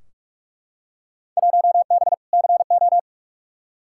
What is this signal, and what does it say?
Morse code, heard as a keyed tone, about 42 wpm, 700 Hz. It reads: 1B CO
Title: Field Day Exchanges - 45wpm